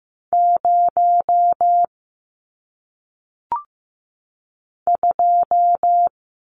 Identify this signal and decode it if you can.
Morse code, heard as a keyed tone, about 15 wpm, 700 Hz. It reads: 0 2
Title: Numbers - 1-Digit 15wpm